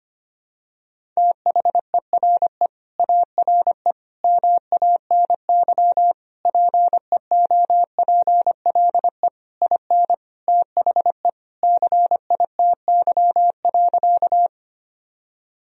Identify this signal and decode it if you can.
Morse code, heard as a keyed tone, about 25 wpm, 700 Hz. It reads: THERE ARE MANY PEOPLE IN THE CITY.
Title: Sentences from Top 1000 Words - Review Part 1 - 25wpm